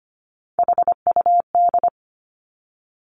Morse code, heard as a keyed tone, about 25 wpm, 700 Hz. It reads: HUD
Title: Three Challenging Letter-Number Combinations 25wpm